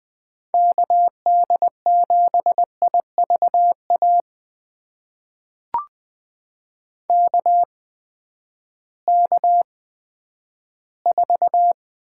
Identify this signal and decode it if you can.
Morse code, heard as a keyed tone, about 20 wpm, spaced as for 11 wpm, 700 Hz. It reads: KD7IVA KK4
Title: USA Call Signs - 6 Characters - 8x Character Spacing - 20wpm